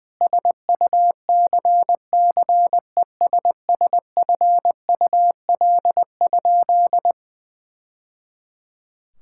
Morse code, heard as a keyed tone, about 21 wpm, 700 Hz. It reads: SUCCESSFUL?
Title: Sentences from Top 1000 Words - Review Part 5 - 20wpm